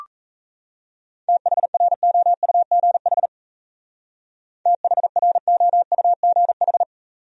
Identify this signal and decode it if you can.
Morse code, heard as a keyed tone, about 35 wpm, 700 Hz. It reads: THROUGH THROUGH
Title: Speed-Racing Top 200 Words 28wpm